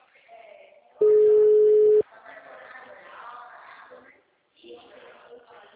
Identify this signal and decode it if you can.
Ringback tone (Europe, 425 Hz), with speech.
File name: out- -2008- - - .37